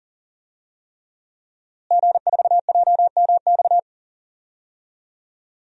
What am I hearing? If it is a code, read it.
Morse code, heard as a keyed tone, about 40 wpm, 700 Hz. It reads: G4JMX
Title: DXCC-100 Call Signs - 5 Characters - Repeated 3x - 40wpm